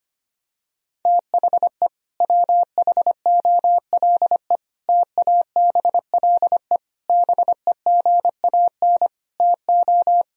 Morse code, heard as a keyed tone, about 25 wpm, 700 Hz. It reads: THE WHOLE TABLE BEGAN TO
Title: Sentences from Top 1000 Words - Review Part 2 - 25wpm